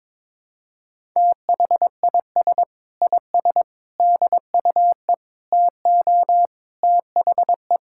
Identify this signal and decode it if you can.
Morse code, heard as a keyed tone, about 22 wpm, 700 Hz. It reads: THIS IS DUE TO THE
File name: practice / 22wpm / Sets of 7 Words - 22wpm